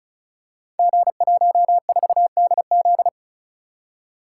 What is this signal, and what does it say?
Morse code, heard as a keyed tone, about 36 wpm, 700 Hz. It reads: G14DZ
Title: Code Groups - 5 Letter:Numbers - 35wpm